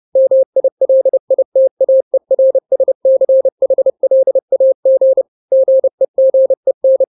Morse code, heard as a keyed tone, about 30 wpm, 530 Hz. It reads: MILITAERSCHLAG GEGEN
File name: 29wpm